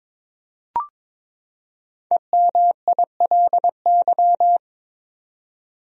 Morse code, heard as a keyed tone, about 22 wpm, 700 Hz. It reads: EMILY
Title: USA Common Names - 22wpm